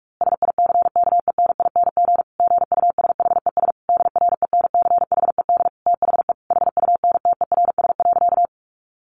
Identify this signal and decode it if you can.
Morse code, heard as a keyed tone, about 44 wpm, 700 Hz. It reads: SICKENING GUSHES DRENCHED THE HUNTERS.
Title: A Sound Of Thunder 45wpm